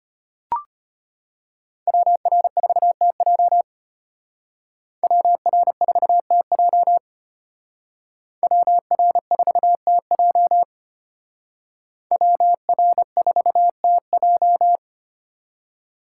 Morse code, heard as a keyed tone, about 32 wpm, 700 Hz. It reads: WR4TJ WR4TJ WR4TJ WR4TJ